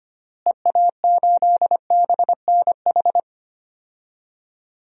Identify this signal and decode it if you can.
Morse code, heard as a keyed tone, about 25 wpm, 700 Hz. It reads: EA8BNH
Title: DXCC-100 Call Signs - 6 Characters - Repeated 3x - 25wpm